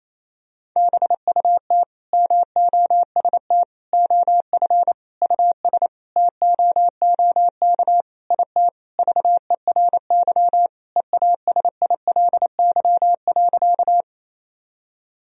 Morse code, heard as a keyed tone, about 28 wpm, 700 Hz. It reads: BUT MOST OF US TOOK IT VERY EASILY.
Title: Sentences from Top 1000 Words - Review Part 1 - 28wpm